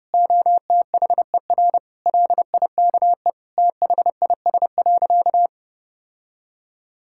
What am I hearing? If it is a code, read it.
Morse code, heard as a keyed tone, about 30 wpm, 700 Hz. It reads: OTHER LIKE THIS.
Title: Sentences from Top 1000 Words - Review Part 6 - 30wpm